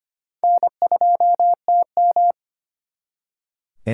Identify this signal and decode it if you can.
Morse code, heard as a keyed tone, about 32 wpm, 700 Hz. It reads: N2TM